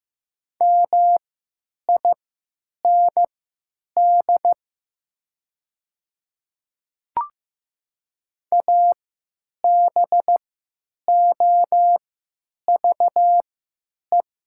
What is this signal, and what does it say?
Morse code, heard as a keyed tone, about 15 wpm, spaced as for 9 wpm, 700 Hz. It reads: MIND ABOVE
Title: 3x Character Spacing - Top 500 Words - 15wpm